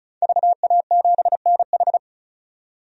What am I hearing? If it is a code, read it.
Morse code, heard as a keyed tone, about 34 wpm, 700 Hz. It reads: VA7NH
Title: DXCC-100 Call Signs - 5 Characters - 35wpm